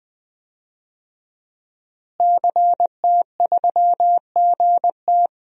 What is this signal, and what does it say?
Morse code, heard as a keyed tone, about 20 wpm, 700 Hz. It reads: CT3GT
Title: DXCC-100 Call Signs - 5 Characters - Repeated 2x - 20wpm